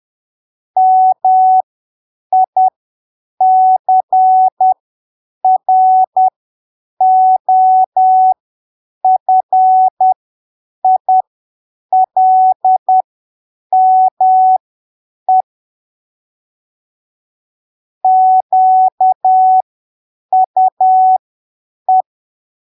Morse code, heard as a keyed tone, about 10 wpm, spaced as for 7 wpm, 750 Hz. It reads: MICROFILME QUE